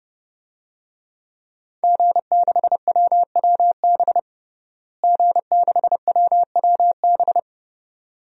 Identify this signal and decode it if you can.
Morse code, heard as a keyed tone, about 32 wpm, 700 Hz. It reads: G6WWB G6WWB
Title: DXCC-100 Call Signs - 5 Characters - Repeated 2x - 30wpm